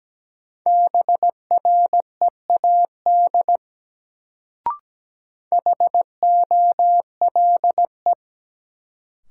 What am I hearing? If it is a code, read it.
Morse code, heard as a keyed tone, about 17 wpm, 700 Hz. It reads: BREAD HOLE
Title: Sentences from Top 1600 Words - New Word Drill Alt - 17wpm